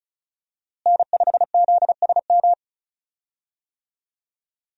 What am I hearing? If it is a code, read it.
Morse code, heard as a keyed tone, about 36 wpm, 700 Hz. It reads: N5ZSM
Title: USA Call Signs - 5 Characters - Repeated 3x - 35wpm